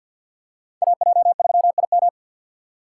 Morse code, heard as a keyed tone, about 47 wpm, 700 Hz. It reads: AJ3IM
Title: USA Call Signs - 5 Characters - 3x Character Spacing - 50wpm